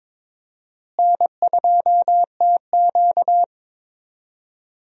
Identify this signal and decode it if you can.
Morse code, heard as a keyed tone, about 22 wpm, 700 Hz. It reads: N2TQ
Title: USA Call Signs - 4 Characters - Repeated 2x - 22wpm